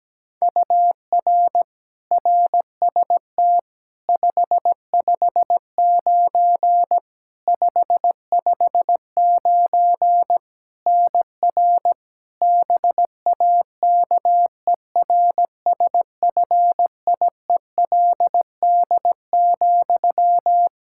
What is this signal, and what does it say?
Morse code, heard as a keyed tone, about 17 wpm, 700 Hz. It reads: UR RST 559 559 NR BAKERSFIELD,
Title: Practice QSO - Exchange 1 - Beginning - 17wpm